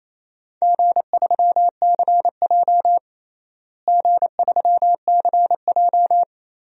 Morse code, heard as a keyed tone, about 28 wpm, 700 Hz. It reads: G3CJ G3CJ
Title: DXCC-100 Call Signs - 4 Characters - Repeated 3x - 28wpm